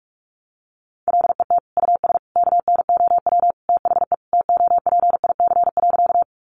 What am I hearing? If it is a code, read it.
Morse code, heard as a keyed tone, about 44 wpm, 700 Hz. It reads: LET US KNOW THE TOPIC.
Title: Sentences from Top 1200 Words 45wpm